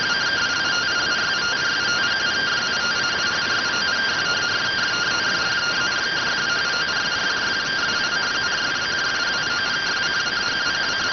Two-tone teleprinter signal, about 170 Hz shift.